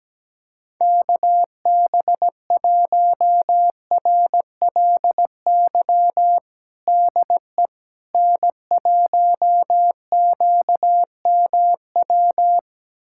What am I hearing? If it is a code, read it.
Morse code, heard as a keyed tone, about 17 wpm, 700 Hz. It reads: KB1RLY DE N1QMW
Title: USA Callsign DE Callsign 17wpm